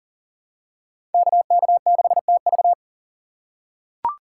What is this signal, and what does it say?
Morse code, heard as a keyed tone, about 37 wpm, 700 Hz. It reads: KK6TV
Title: K1USN SST QSO Practice - 40wpm